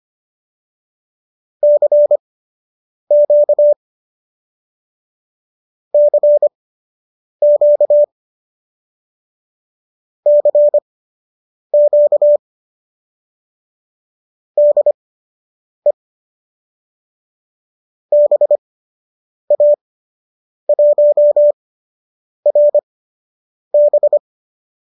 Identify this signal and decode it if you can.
Morse code, heard as a keyed tone, about 25 wpm, spaced as for 8 wpm, 590 Hz. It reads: CQ CQ CQ DE BA1RB